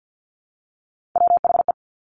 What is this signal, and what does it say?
Morse code, heard as a keyed tone, about 50 wpm, 700 Hz. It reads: WHE
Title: Top Three Letter Combinations in Words 50wpm